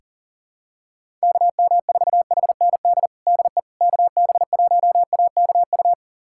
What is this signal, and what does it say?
Morse code, heard as a keyed tone, about 40 wpm, 700 Hz. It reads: KM4HND DE KB1AKU
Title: USA Callsign DE Callsign 40wpm